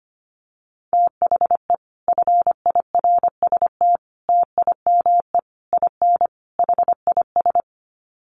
Morse code, heard as a keyed tone, about 25 wpm, 700 Hz. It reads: THE FIRST TIME IN HIS
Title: Sets of 5 Words Encore2 25wpm